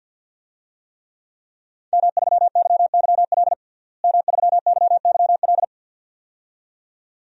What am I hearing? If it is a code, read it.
Morse code, heard as a keyed tone, about 49 wpm, 700 Hz. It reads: M3YYL M3YYL
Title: DXCC-100 Call Signs - 5 Characters - Repeated 2x - 50wpm